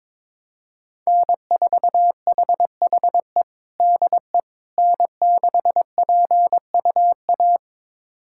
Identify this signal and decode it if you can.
Morse code, heard as a keyed tone, about 22 wpm, 700 Hz. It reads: N4HHE DE N6PUA
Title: USA Callsign DE Callsign - 5 Characters - 22wpm